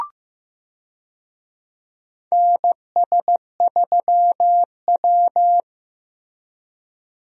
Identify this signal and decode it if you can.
Morse code, heard as a keyed tone, about 15 wpm, 700 Hz. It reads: NS3W
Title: USA Call Signs - 4 Characters - Repeated 3x - 15wpm